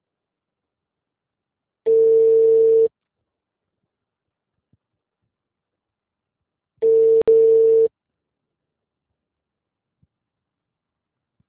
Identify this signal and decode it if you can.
Ringback tone (China, 450 Hz).